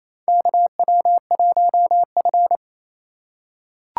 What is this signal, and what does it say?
Morse code, heard as a keyed tone, about 28 wpm, 700 Hz. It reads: KW1F
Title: USA Call Signs - 4 Characters - 2x Character Spacing - 28wpm